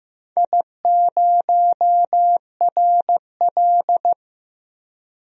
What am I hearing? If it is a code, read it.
Morse code, heard as a keyed tone, about 15 wpm, 700 Hz. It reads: I0RL
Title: DXCC-100 Call Signs - 4 Characters - 15wpm